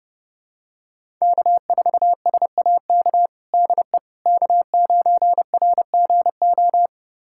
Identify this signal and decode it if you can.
Morse code, heard as a keyed tone, about 30 wpm, 700 Hz. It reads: K4SAK DE K9RGO
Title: USA Callsign DE Callsign - 5 Characters - 30wpm